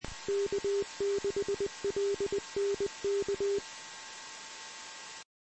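Morse code, heard as a keyed tone, about 20 wpm, 400 Hz. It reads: K6LNK
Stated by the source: Morse code recorded on the 2 m band at 146.115 MHz.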